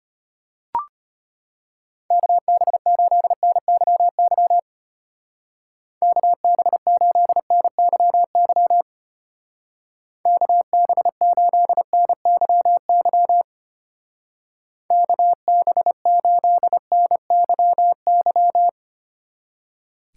Morse code, heard as a keyed tone, about 32 wpm, 700 Hz. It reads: KB8NYY KB8NYY KB8NYY KB8NYY